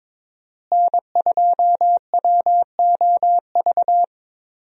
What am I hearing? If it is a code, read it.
Morse code, heard as a keyed tone, about 22 wpm, 700 Hz. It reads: N2WOV
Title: USA Call Signs - 5 Characters - Repeated 2x - 22wpm